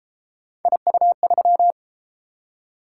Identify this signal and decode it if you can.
Morse code, heard as a keyed tone, about 31 wpm, 700 Hz. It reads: IU3